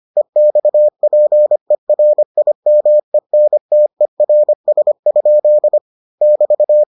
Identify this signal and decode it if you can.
Morse code, heard as a keyed tone, about 25 wpm, 600 Hz. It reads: EXPERIMENTERS? <BT>